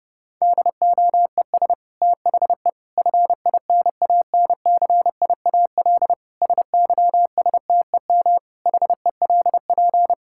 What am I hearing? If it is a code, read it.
Morse code, heard as a keyed tone, about 30 wpm, 700 Hz. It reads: DOES THE FINANCIAL SYSTEM HELP
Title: Sentences from Top 1000 Words - Review Part 5 - 30wpm